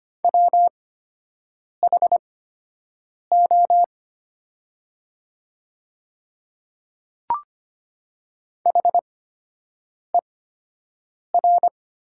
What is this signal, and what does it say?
Morse code, heard as a keyed tone, about 25 wpm, spaced as for 7 wpm, 700 Hz. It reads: WHO HER
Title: 8x Character Spacing - Top 100 Words - 25wpm